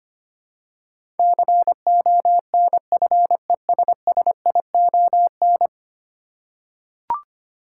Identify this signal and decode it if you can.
Morse code, heard as a keyed tone, about 25 wpm, 700 Hz. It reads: CONFESSION
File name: Ten Letter Words - 25wpm